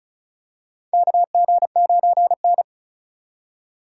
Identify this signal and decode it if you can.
Morse code, heard as a keyed tone, about 36 wpm, 700 Hz. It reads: KG9N
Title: USA Call Signs - 4 Characters - 4x Character Spacing - 35wpm